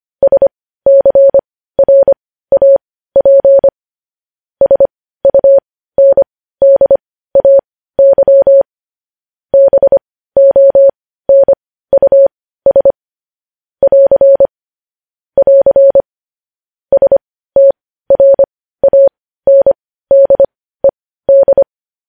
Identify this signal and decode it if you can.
Morse code, heard as a keyed tone, about 25 wpm, spaced as for 15 wpm, 550 Hz. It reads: SCRAP SUNDAY BONUS <AR> <AR> STRANDED
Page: news headlines in morse code at 15 wpm